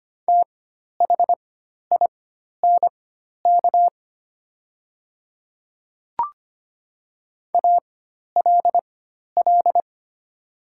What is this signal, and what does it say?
Morse code, heard as a keyed tone, about 25 wpm, spaced as for 12 wpm, 700 Hz. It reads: THINK ALL
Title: 4x Character Spacing - Top 300 Words - 25wpm